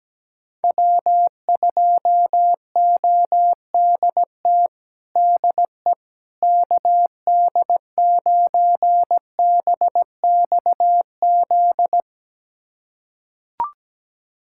Morse code, heard as a keyed tone, about 17 wpm, 700 Hz. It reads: W2ODT DE KD9BXZ
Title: USA Callsign DE Callsign 17wpm